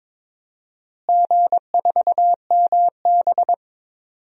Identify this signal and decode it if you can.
Morse code, heard as a keyed tone, about 22 wpm, 700 Hz. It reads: G4MB
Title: DXCC-100 Call Signs - 4 Characters - Repeated 2x - 22wpm